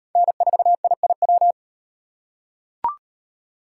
Morse code, heard as a keyed tone, about 34 wpm, 700 Hz. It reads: N4IIW